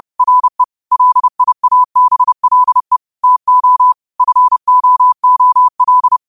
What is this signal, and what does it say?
Morse code, heard as a keyed tone, about 30 wpm, 1000 Hz. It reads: RE LIABLE TO FOOL